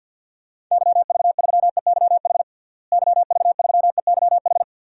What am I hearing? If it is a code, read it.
Morse code, heard as a keyed tone, about 47 wpm, 700 Hz. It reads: YV3EYH YV3EYH
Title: DXCC-100 Call Signs - 6 Characters - Repeated 2x - 50wpm